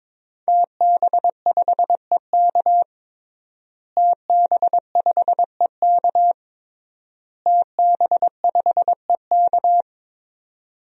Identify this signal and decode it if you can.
Morse code, heard as a keyed tone, about 22 wpm, 700 Hz. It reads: TB5EK TB5EK TB5EK
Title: DXCC-100 Call Signs - 5 Characters - Repeated 3x - 22wpm